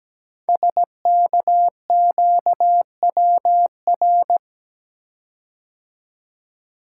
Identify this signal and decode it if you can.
Morse code, heard as a keyed tone, about 17 wpm, 700 Hz. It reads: SKQWR
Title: Code Groups - 5 Letter:Numbers - 17wpm